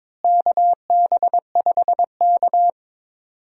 Morse code, heard as a keyed tone, about 22 wpm, 700 Hz. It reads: KB5K
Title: USA Call Signs - 4 Characters - Repeated 3x - 22wpm